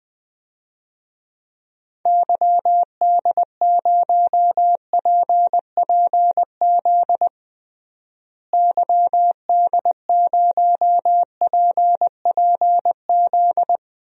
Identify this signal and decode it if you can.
Morse code, heard as a keyed tone, about 20 wpm, 700 Hz. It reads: YD0PPZ YD0PPZ
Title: DXCC-100 Call Signs - 6 Characters - Repeated 3x - 20wpm